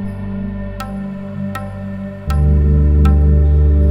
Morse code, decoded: none